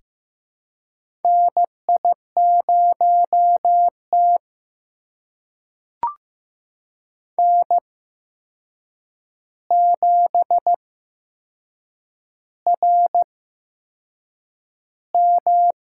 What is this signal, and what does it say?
Morse code, heard as a keyed tone, about 15 wpm, spaced as for 6 wpm, 700 Hz. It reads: NI0T N7RM
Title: USA Call Signs - 8x Character Spacing - 15wpm